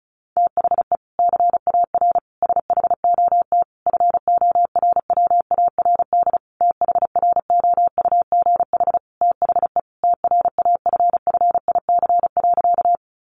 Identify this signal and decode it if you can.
Morse code, heard as a keyed tone, about 34 wpm, 700 Hz. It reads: THE CAR SHOT FORWARD THROUGH THE TRAFFIC.
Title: Sentences from Top 1300 Words 35wpm